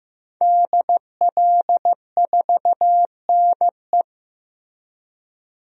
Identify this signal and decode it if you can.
Morse code, heard as a keyed tone, about 15 wpm, 700 Hz. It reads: DL4NE
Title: DXCC-100 Call Signs - 5 Characters - 15wpm